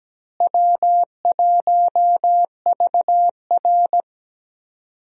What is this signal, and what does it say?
Morse code, heard as a keyed tone, about 17 wpm, 700 Hz. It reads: W1VR
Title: USA Call Signs - 4 Characters - Repeated 2x - 17wpm